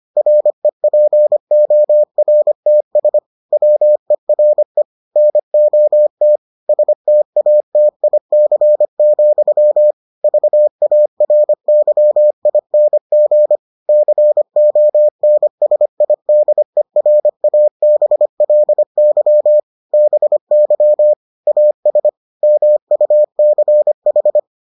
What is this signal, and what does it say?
Morse code, heard as a keyed tone, about 25 wpm, 600 Hz. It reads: REPORTS WERE NOT STATIC, VARYING CONSIDERABLY BY AS MUCH